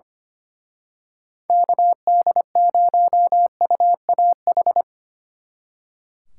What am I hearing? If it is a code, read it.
Morse code, heard as a keyed tone, about 28 wpm, 700 Hz. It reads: KD0UAH
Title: Speed Racing - USA Call Signs 25wpm